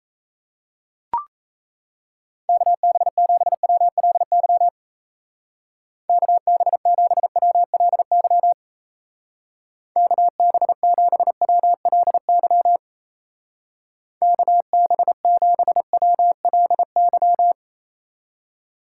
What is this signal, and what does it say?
Morse code, heard as a keyed tone, about 36 wpm, 700 Hz. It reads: KB7WLY KB7WLY KB7WLY KB7WLY